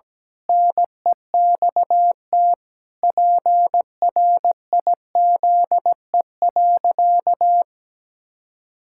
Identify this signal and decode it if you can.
Morse code, heard as a keyed tone, about 17 wpm, 700 Hz. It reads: NEXT PRIZE.
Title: Pangram Challenge v2 - 17wpm